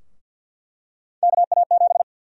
Morse code, heard as a keyed tone, about 48 wpm, 700 Hz. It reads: KA7
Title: USA Call Sign Prefixes - Repeated 2x - 50wpm